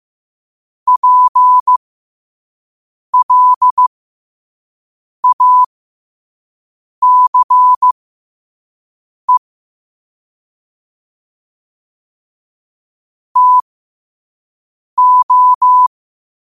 Morse code, heard as a keyed tone, about 15 wpm, spaced as for 5 wpm, 1000 Hz. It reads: PLACE TO